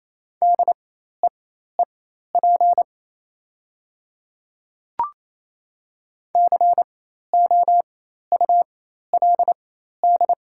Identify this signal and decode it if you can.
Morse code, heard as a keyed tone, about 28 wpm, spaced as for 13 wpm, 700 Hz. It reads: DEEP COULD
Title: 4x Character Spacing - Top 500 Words - 28wpm